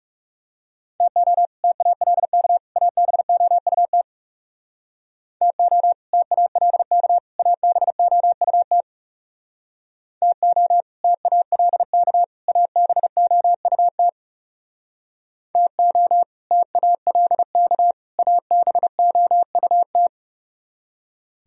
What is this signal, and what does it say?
Morse code, heard as a keyed tone, about 36 wpm, 700 Hz. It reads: TO TALK ABOUT TO TALK ABOUT TO TALK ABOUT TO TALK ABOUT
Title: Speed-Racing - Sets of 3 Words 30wpm